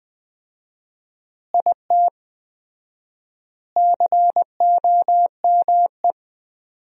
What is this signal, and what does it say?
Morse code, heard as a keyed tone, about 21 wpm, 700 Hz. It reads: IT COME
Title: 4x Word-Spacing - Sentences from Top 100 Words - 20wpm